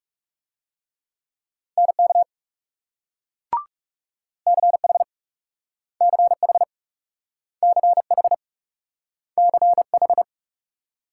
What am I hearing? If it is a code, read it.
Morse code, heard as a keyed tone, about 38 wpm, 700 Hz. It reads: NK CH CH CH CH